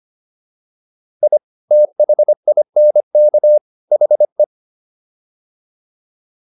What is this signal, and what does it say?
Morse code, heard as a keyed tone, about 25 wpm, 600 Hz. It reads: I THINK HE